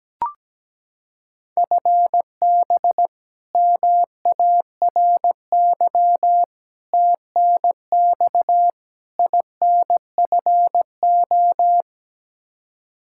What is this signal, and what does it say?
Morse code, heard as a keyed tone, about 17 wpm, 700 Hz. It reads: FB MARY TNX INFO
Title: Practice QSO - Exchange 3 - More Info - 17wpm